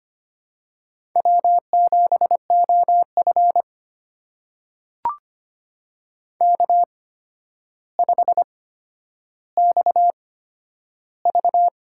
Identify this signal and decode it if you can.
Morse code, heard as a keyed tone, about 25 wpm, spaced as for 11 wpm, 700 Hz. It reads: W7OF K5XV
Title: USA Call Signs - 4 Characters - 8x Character Spacing - 25wpm